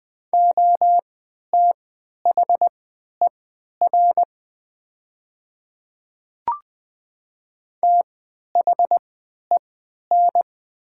Morse code, heard as a keyed tone, about 20 wpm, spaced as for 11 wpm, 700 Hz. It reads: OTHER THEN
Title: 3x Character Spacing - Top 200 Words - 20wpm